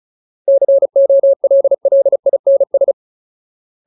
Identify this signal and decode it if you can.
Morse code, heard as a keyed tone, about 34 wpm, 550 Hz. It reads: COLLINS
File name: Ultimate Word List / 35WPM Ultimate